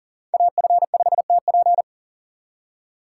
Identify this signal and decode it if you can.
Morse code, heard as a keyed tone, about 40 wpm, 700 Hz. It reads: AF5TP
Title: USA Call Signs - 5 Characters - 2x Character Spacing - 40wpm